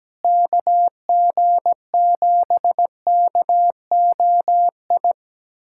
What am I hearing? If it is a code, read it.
Morse code, heard as a keyed tone, about 17 wpm, 700 Hz. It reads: KG7KOI
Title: USA Callsign DE Callsign - 6 Characters - 17wpm